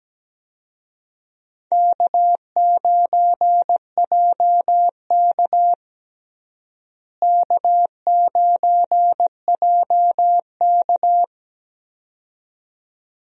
Morse code, heard as a keyed tone, about 17 wpm, 700 Hz. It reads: K9JK K9JK
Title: USA Call Signs - 4 Characters - Repeated 2x - 17wpm